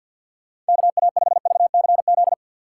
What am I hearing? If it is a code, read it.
Morse code, heard as a keyed tone, about 54 wpm, 700 Hz. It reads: KA5VCZ
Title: USA Call Signs - 6 Characters - 6x Character Spacing - 50wpm